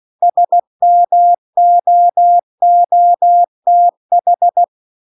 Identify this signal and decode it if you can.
Morse code, heard as a keyed tone, about 16 wpm, 700 Hz. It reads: SMOOTH
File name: CW-The-Big-Twoheaded-River-2-16WPM